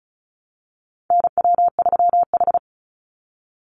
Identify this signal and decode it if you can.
Morse code, heard as a keyed tone, about 34 wpm, 700 Hz. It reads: NW3H
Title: USA Call Signs 35wpm